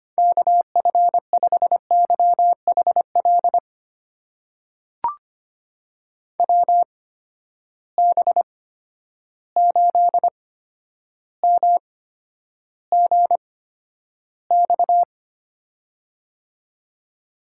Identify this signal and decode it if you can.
Morse code, heard as a keyed tone, about 25 wpm, spaced as for 11 wpm, 700 Hz. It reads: KF5YHL WB8MGX
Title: USA Call Signs - 6 Characters - 8x Character Spacing - 25wpm